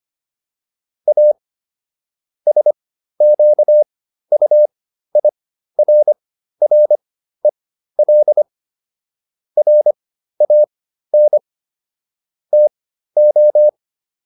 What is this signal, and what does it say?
Morse code, heard as a keyed tone, about 25 wpm, spaced as for 13 wpm, 600 Hz. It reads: A SQUIRREL RAN TO